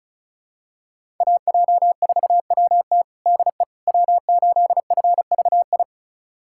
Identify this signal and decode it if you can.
Morse code, heard as a keyed tone, about 36 wpm, 700 Hz. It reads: AJ4WT DE W8FVI
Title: USA Callsign DE Callsign - 5 Characters - 35wpm